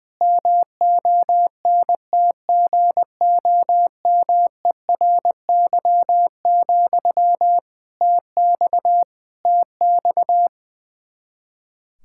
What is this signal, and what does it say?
Morse code, heard as a keyed tone, about 20 wpm, 700 Hz. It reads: MONTGOMERY, TX TX
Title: Practice QSO - Exchange 1 - Beginning - 20wpm